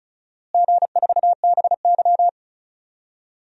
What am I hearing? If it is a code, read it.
Morse code, heard as a keyed tone, about 36 wpm, 700 Hz. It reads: G4BY
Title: DXCC-100 Call Signs - 4 Characters - Repeated 3x - 35wpm